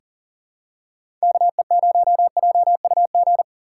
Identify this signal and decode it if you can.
Morse code, heard as a keyed tone, about 40 wpm, 700 Hz. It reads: KE0JUG
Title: USA Call Signs - 4x Character Spacing - 40wpm